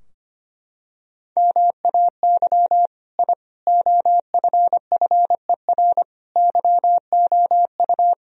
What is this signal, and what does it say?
Morse code, heard as a keyed tone, about 25 wpm, 700 Hz. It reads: MAY I OFFER YOU
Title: Sentences from Top 1000 Words - Review Part 4 - 25wpm